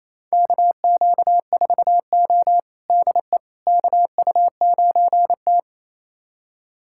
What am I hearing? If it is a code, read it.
Morse code, heard as a keyed tone, about 28 wpm, 700 Hz. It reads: KQ4O DE KU9T
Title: USA Callsign DE Callsign - 4 Characters - 28wpm